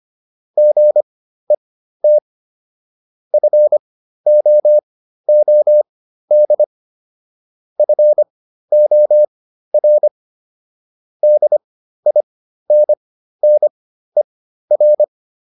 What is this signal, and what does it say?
Morse code, heard as a keyed tone, about 25 wpm, spaced as for 13 wpm, 600 Hz. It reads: GET FOOD FOR DINNER